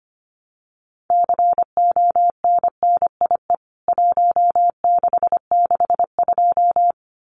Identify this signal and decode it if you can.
Morse code, heard as a keyed tone, about 25 wpm, 700 Hz. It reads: CONNIE 1662
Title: CWT Exchange 25wpm